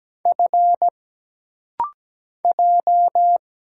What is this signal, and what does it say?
Morse code, heard as a keyed tone, about 17 wpm, 700 Hz. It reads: F J
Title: Single Letter-Number - Rapid-Fire 17wpm